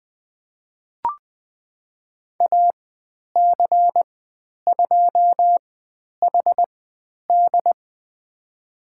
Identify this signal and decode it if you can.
Morse code, heard as a keyed tone, about 20 wpm, spaced as for 10 wpm, 700 Hz. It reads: AC2HD
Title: 095 Lesson 16g - Letter c - Call Signs 20wpm